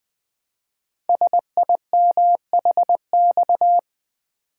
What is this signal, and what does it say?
Morse code, heard as a keyed tone, about 20 wpm, 700 Hz. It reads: SIMHX
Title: Code Groups - 5 Letters - 20wpm